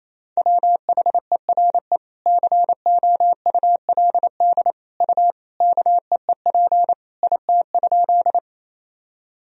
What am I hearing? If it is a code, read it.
Morse code, heard as a keyed tone, about 28 wpm, 700 Hz. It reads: WHERE COULD U KEEP IT?
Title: Sentences from Top 300 Words and Abbreviations 28wpm